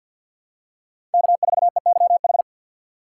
Morse code, heard as a keyed tone, about 52 wpm, 700 Hz. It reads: K4EYH
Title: USA Call Signs - 5 Characters - 2x Character Spacing - 50wpm